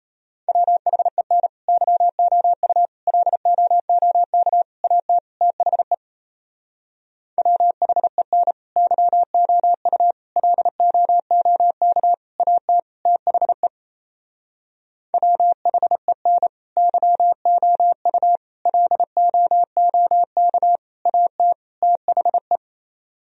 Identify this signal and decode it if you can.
Morse code, heard as a keyed tone, about 32 wpm, 700 Hz. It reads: WHEN YOU LOOK AT THE WHEN YOU LOOK AT THE WHEN YOU LOOK AT THE
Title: Speed-Racing - Sets of 5 Words 28wpm